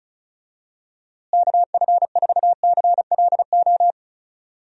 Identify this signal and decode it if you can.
Morse code, heard as a keyed tone, about 36 wpm, 700 Hz. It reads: KF4CLO
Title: USA Call Signs - 6 Characters - Repeated 3x - 35wpm